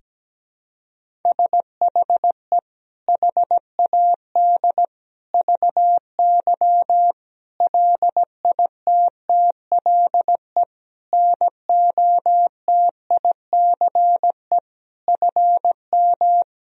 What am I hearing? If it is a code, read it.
Morse code, heard as a keyed tone, about 17 wpm, 700 Hz. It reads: SHE HAD VY LITTLE NOTICE FM
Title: Sentences from Top 400 Words and Abbreviations 17wpm